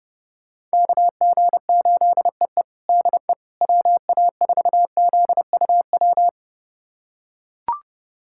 Morse code, heard as a keyed tone, about 32 wpm, 700 Hz. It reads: KG8EE DE WA4ZUW
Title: USA Callsign DE Callsign 30wpm